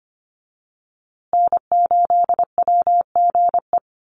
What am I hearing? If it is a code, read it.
Morse code, heard as a keyed tone, about 25 wpm, 700 Hz. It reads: N8WGE
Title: USA Call Signs Repeated 3x 25wpm